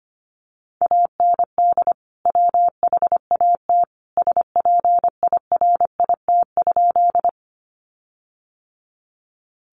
Morse code, heard as a keyed tone, about 25 wpm, 700 Hz. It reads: AND WHAT SPIRIT?
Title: Sentences from Top 1400 Words 25wpm